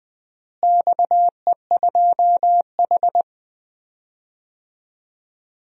Morse code, heard as a keyed tone, about 19 wpm, 700 Hz. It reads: XE2H
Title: DXCC-100 Call Signs - 4 Characters - Repeated 2x - 20wpm